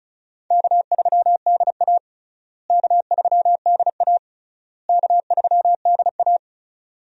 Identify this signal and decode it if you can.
Morse code, heard as a keyed tone, about 36 wpm, 700 Hz. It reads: K3DA K3DA K3DA
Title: USA Call Signs - 4 Characters - Repeated 3x - 35wpm